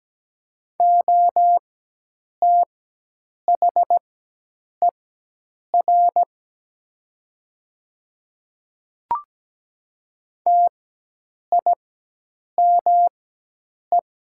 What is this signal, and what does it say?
Morse code, heard as a keyed tone, about 17 wpm, spaced as for 8 wpm, 700 Hz. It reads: OTHER TIME
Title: 4x Character Spacing - Top 100 Words - 17wpm